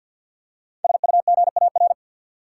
Morse code, heard as a keyed tone, about 53 wpm, 700 Hz. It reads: SUGAR
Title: Five Letter Words - 50wpm